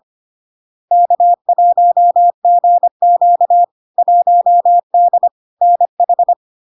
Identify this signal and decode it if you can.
Morse code, heard as a keyed tone, about 25 wpm, 700 Hz. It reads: K1GQ 1D NH
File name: Log-in-CW-25wpm-700-Hz